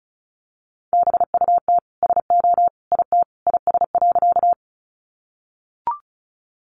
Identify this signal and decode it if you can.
Morse code, heard as a keyed tone, about 34 wpm, 700 Hz. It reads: BUT SO IT IS.
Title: Sentences from Top 100 Words 35wpm